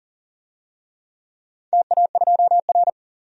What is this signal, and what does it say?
Morse code, heard as a keyed tone, about 40 wpm, 700 Hz. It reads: TA2R
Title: DXCC-100 Call Signs - 4 Characters - 40wpm